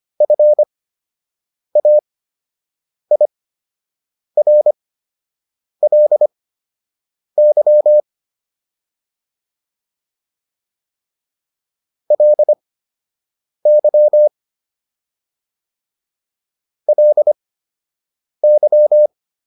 Morse code, heard as a keyed tone, about 25 wpm, spaced as for 7 wpm, 600 Hz. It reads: FAIRLY LY LY
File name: LY6 07